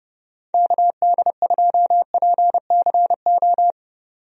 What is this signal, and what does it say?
Morse code, heard as a keyed tone, about 30 wpm, 700 Hz. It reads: KD2PCO
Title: USA Call Signs - 6 Characters - 30wpm